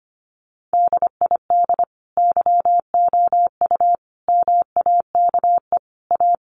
Morse code, heard as a keyed tone, about 25 wpm, 700 Hz. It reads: DID YOU MAKE A
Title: Sentences from Top 1300 Words 25wpm